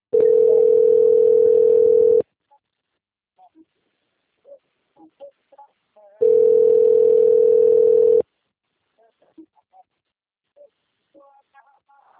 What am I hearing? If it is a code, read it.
Ringback tone (North America), 440 + 480 Hz, with speech.
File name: out- - - - - .37419